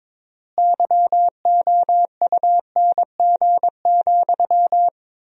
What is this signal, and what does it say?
Morse code, heard as a keyed tone, about 22 wpm, 700 Hz. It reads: YOUNG,
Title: 4x Word-Spacing - Sentences from Top 400 Words - 22wpm